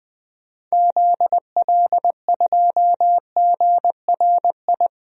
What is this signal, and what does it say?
Morse code, heard as a keyed tone, about 21 wpm, 700 Hz. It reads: ZL2GRI
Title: DXCC-100 Call Signs - 6 Characters - 20wpm